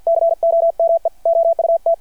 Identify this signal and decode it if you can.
Morse code, heard as a keyed tone, about 51 wpm, 660 Hz. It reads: COME OUT
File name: archived / CRSnet